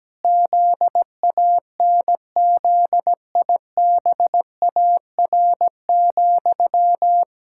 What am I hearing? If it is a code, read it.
Morse code, heard as a keyed tone, about 17 wpm, 700 Hz. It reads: ZANZIBAR,
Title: Pangram Challenge v2 - 17wpm